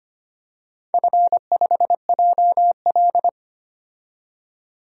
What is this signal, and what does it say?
Morse code, heard as a keyed tone, about 25 wpm, 700 Hz. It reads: F5JL
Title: DXCC-100 Call Signs - 4 Characters - 25wpm